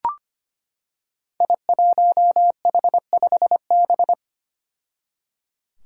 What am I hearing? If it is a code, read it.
Morse code, heard as a keyed tone, about 25 wpm, 700 Hz. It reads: I1H5B
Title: Five Challenging Letter-Number Combinations 25wpm